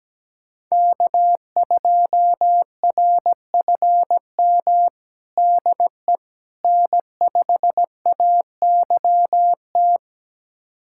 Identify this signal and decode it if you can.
Morse code, heard as a keyed tone, about 17 wpm, 700 Hz. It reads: K2RFM DE N5AYT
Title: USA Callsign DE Callsign - 5 Characters - 17wpm